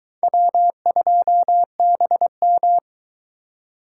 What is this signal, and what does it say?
Morse code, heard as a keyed tone, about 21 wpm, 700 Hz. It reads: W2BM